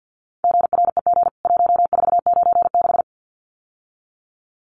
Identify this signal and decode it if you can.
Morse code, heard as a keyed tone, about 50 wpm, 700 Hz. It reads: GREG 1496
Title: CWT Exchange 50wpm